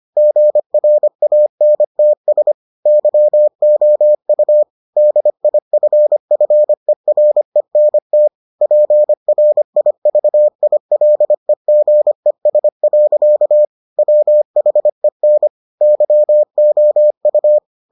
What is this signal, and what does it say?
Morse code, heard as a keyed tone, about 25 wpm, 600 Hz. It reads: GRANTS YOU DIFFERENT PRIVILEGES. WHEN YOU